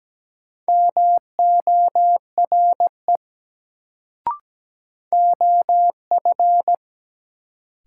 Morse code, heard as a keyed tone, about 17 wpm, 700 Hz. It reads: MORE OF
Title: Sentences from Top 100 Words - New Word Drill - 17wpm